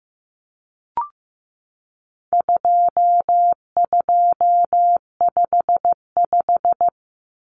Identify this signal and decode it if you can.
Morse code, heard as a keyed tone, about 15 wpm, 700 Hz. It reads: 2255
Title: Numbers - 4-Digit 15wpm